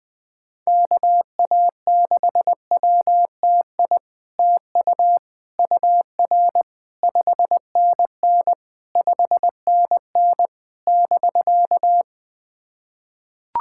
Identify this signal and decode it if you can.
Morse code, heard as a keyed tone, about 20 wpm, 700 Hz. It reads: KA6WTI TU UR 5NN 5NN <BK>
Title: SOTA QSO Practice 20wpm